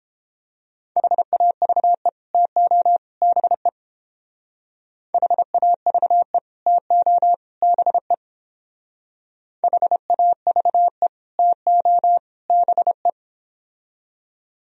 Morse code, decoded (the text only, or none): HAVE TO BE HAVE TO BE HAVE TO BE